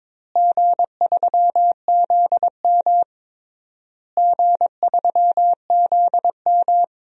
Morse code, heard as a keyed tone, about 22 wpm, 700 Hz. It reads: G3ZM G3ZM
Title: DXCC-100 Call Signs - 4 Characters - Repeated 3x - 22wpm